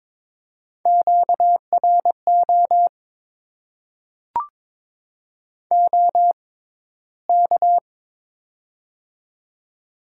Morse code, heard as a keyed tone, about 22 wpm, spaced as for 13 wpm, 700 Hz. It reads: QRO OK
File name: Top 90 QSO Elements - 6x Character Spacing - 22wpm